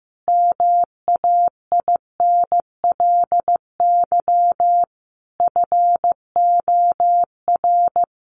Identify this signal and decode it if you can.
Morse code, heard as a keyed tone, about 15 wpm, 700 Hz. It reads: MAINLY FOR
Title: Sentences from Top 1400 Words 15wpm